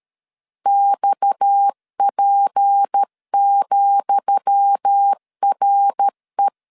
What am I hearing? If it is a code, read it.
Morse code, heard as a keyed tone, about 13 wpm, 800 Hz. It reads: XP,RE
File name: LEZIONI / 10 SIGNS / 10D COMA 12wpm 2544